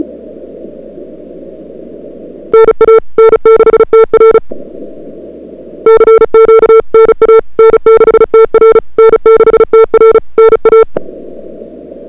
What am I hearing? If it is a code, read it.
Morse code, heard as a keyed tone, about 34 wpm, 440 Hz. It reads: NA N6TR CQ NA N6TR N6TR NA
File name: n6tr / Fab2007CWSprintHour1